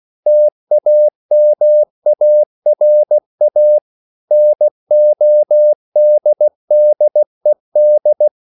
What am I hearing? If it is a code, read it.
Morse code, heard as a keyed tone, about 16 wpm, 600 Hz. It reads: TAMARA NODDED